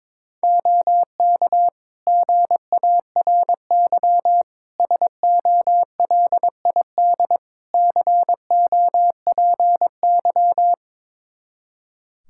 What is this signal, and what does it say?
Morse code, heard as a keyed tone, about 22 wpm, 700 Hz. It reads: OK GARY SOLID COPY
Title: Practice QSO - Exchange 3 - More Info - 22wpm